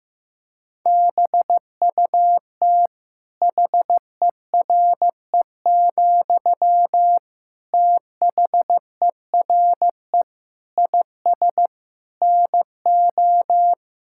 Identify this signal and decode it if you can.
Morse code, heard as a keyed tone, about 15 wpm, 700 Hz. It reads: BUT HERE, THERE IS NO
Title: Sentences from Top 1000 Words - Review Part 6 - 15wpm